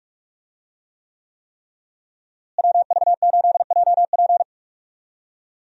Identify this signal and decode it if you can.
Morse code, heard as a keyed tone, about 41 wpm, 700 Hz. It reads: WV8JP
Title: USA Call Signs - 5 Characters - Repeated 3x - 45wpm